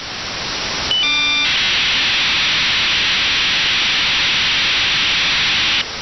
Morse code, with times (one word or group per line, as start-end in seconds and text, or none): none